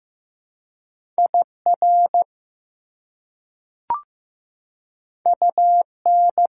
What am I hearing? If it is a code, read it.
Morse code, heard as a keyed tone, about 15 wpm, 700 Hz. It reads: IR UN
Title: Prefixes and Suffixes 15wpm